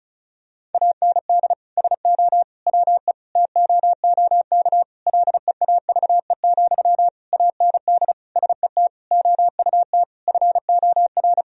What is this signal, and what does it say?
Morse code, heard as a keyed tone, about 36 wpm, 700 Hz. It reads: AND SO WE TOOK LEAVE, AND SET OUT FOR
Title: Sentences from Top 1000 Words - Review Part 1 - 35wpm